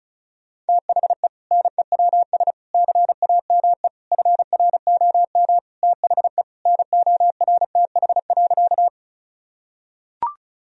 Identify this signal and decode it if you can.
Morse code, heard as a keyed tone, about 34 wpm, 700 Hz. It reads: THE NEWS CAME FROM THE NORTH.
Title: Sentences from Top 1000 Words - Review Part 1 - 35wpm